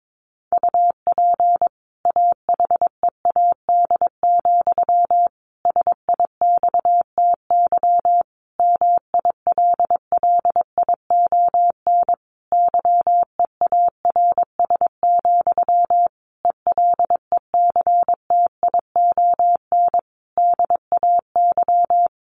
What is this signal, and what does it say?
Morse code, heard as a keyed tone, about 22 wpm, 700 Hz. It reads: UP AHEAD, SIXTY MILLION YEARS, ELECTION DAY
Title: A Sound Of Thunder 22wpm